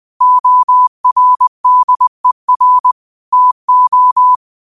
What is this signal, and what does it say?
Morse code, heard as a keyed tone, about 20 wpm, 1000 Hz. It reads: ORDER TO
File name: qotdfiles / 20wpm